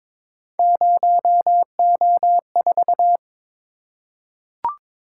Morse code, heard as a keyed tone, about 22 wpm, 700 Hz. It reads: 0O4
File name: practice / 22wpm / Three Letter-Number Combinations - 22wpm